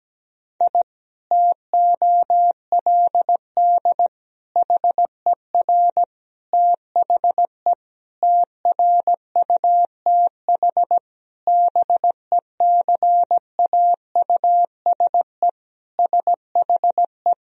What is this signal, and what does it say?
Morse code, heard as a keyed tone, about 17 wpm, 700 Hz. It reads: I TOLD HER THE TRUTH BECAUSE SHE
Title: Sentences from Top 1000 Words - Review Part 3 - 17wpm